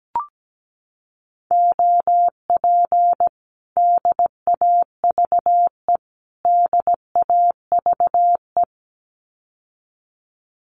Morse code, heard as a keyed tone, about 17 wpm, 700 Hz. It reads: OP DAVE DAVE
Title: Common QSO Phrases Encore 17wpm